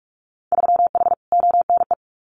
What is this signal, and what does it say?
Morse code, heard as a keyed tone, about 46 wpm, 700 Hz. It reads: 3H ONE
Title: Winter Field Day Exchanges Encore 45wpm